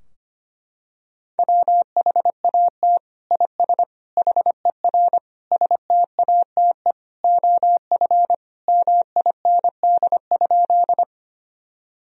Morse code, heard as a keyed tone, about 25 wpm, 700 Hz. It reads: WHAT IS HER STATE OF MIND?
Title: Sentences from Top 1000 Words - Review Part 2 - 25wpm